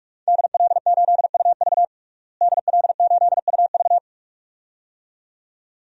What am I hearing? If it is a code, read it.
Morse code, heard as a keyed tone, about 42 wpm, 700 Hz. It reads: DL8UV DL8UV
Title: DXCC-100 Call Signs - 5 Characters - Repeated 2x - 45wpm